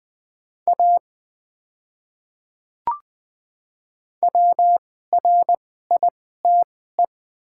Morse code, heard as a keyed tone, about 20 wpm, spaced as for 15 wpm, 700 Hz. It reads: A WRITE
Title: 2x Character Spacing - Top 100 Words - 20wpm